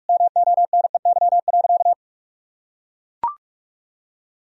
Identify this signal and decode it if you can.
Morse code, heard as a keyed tone, about 41 wpm, 700 Hz. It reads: MONEY.
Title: Sentences from Top 1000 Words - Review Part 2 - 45wpm